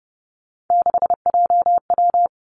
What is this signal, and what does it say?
Morse code, heard as a keyed tone, about 30 wpm, 700 Hz. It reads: 6JW
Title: Three Challenging Letter-Number Combinations 30wpm